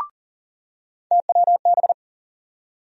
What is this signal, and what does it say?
Morse code, heard as a keyed tone, about 40 wpm, 700 Hz. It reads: TWB
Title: USA Call Sign Postfixes - 3 Letters 40wpm